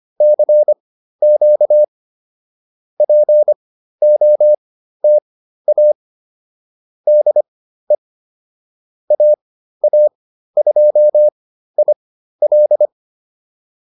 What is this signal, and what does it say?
Morse code, heard as a keyed tone, about 25 wpm, spaced as for 13 wpm, 600 Hz. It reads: CQ POTA DE AA2IL